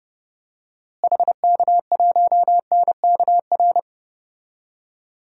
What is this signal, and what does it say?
Morse code, heard as a keyed tone, about 30 wpm, 700 Hz. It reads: HK1NKR
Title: DXCC-100 Call Signs - 6 Characters - Repeated 2x - 30wpm